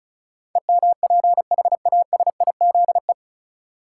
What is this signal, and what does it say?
Morse code, heard as a keyed tone, about 36 wpm, 700 Hz. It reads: EMPHASIZE
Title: Sentences from Top 1500 Words - New Word Drill Alt - 35wpm